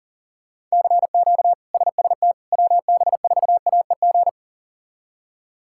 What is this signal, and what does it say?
Morse code, heard as a keyed tone, about 40 wpm, 700 Hz. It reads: CQ SST WB4AEG
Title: K1USN SST QSO Practice - 40wpm